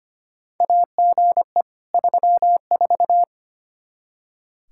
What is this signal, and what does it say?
Morse code, heard as a keyed tone, about 25 wpm, 700 Hz. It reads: AGE 34
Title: K1USN SST QSO Practice 25wpm